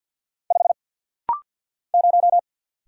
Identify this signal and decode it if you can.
Morse code, heard as a keyed tone, about 49 wpm, 700 Hz. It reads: 5 0
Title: Single Letter-Number - Rapid-Fire 50wpm